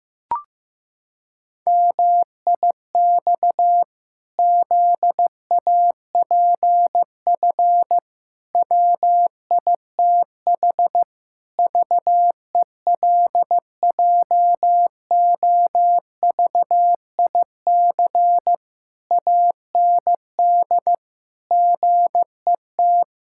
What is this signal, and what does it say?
Morse code, heard as a keyed tone, about 15 wpm, 700 Hz. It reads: MIX ZAPF WITH VELJOVIC AND GET
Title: Pangram Challenge v2 - 15wpm